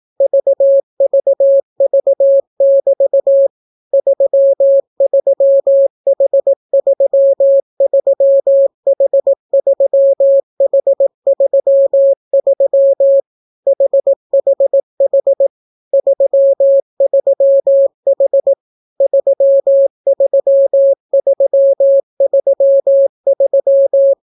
Morse code, heard as a keyed tone, about 18 wpm, 550 Hz. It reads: VVV<BT> 33H33H3H33 HHH 33H 33333